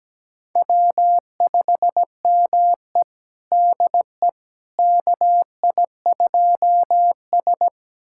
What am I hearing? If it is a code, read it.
Morse code, heard as a keyed tone, about 17 wpm, 700 Hz. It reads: W5ME DE KI2S
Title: USA Callsign DE Callsign - 4 Characters - 17wpm